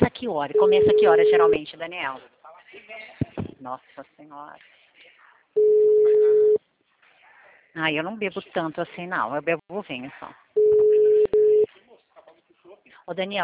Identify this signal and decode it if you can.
Ringback tone (Europe, 425 Hz), with speech.